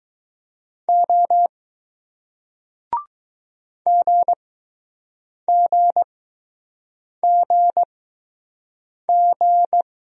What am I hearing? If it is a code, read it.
Morse code, heard as a keyed tone, about 20 wpm, spaced as for 6 wpm, 700 Hz. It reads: O GGGG